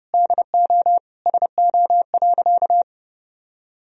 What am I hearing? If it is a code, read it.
Morse code, heard as a keyed tone, about 32 wpm, 700 Hz. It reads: DO SO.
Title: Sentences from Top 1000 Words - Review Part 2 - 30wpm